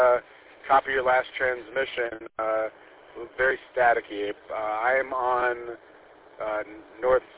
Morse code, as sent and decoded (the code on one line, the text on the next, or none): none